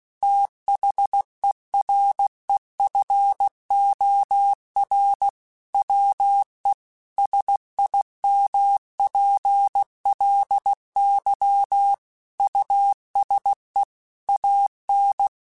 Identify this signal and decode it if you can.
Morse code, heard as a keyed tone, about 16 wpm, 800 Hz. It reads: THEREFOR WE SIMPLY USE AN
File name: Technical Subjects / Technical Subjects-mp3s / Sw Pwr Sup @15